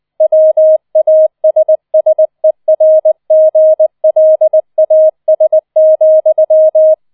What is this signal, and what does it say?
Morse code, heard as a keyed tone, about 20 wpm, 630 Hz. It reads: WASSERGLAS,